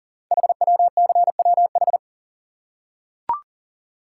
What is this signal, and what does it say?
Morse code, heard as a keyed tone, about 38 wpm, 690 Hz. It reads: HWCWH
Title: Code Groups - 5 Letters - 40wpm